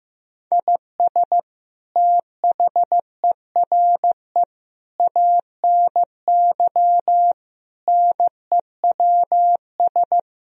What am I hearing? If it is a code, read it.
Morse code, heard as a keyed tone, about 16 wpm, 700 Hz. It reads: IS THERE ANY NEWS
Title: Sentences from Top 1000 Words - Review Part 4 - 15wpm